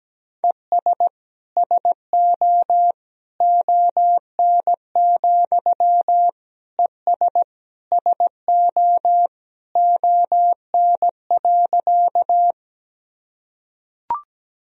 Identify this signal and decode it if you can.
Morse code, heard as a keyed tone, about 17 wpm, 700 Hz. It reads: ES SO ON, ES SO ON.
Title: Sentences from Top 100 Words and Abbreviations 17wpm